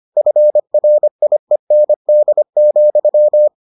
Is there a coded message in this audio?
Yes: Morse code from 0.2 to 3.5 s, heard as a keyed tone, about 25 wpm, 600 Hz. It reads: FRIEND,